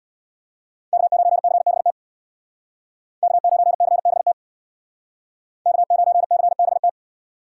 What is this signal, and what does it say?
Morse code, heard as a keyed tone, about 63 wpm, 700 Hz. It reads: K9CBT K9CBT K9CBT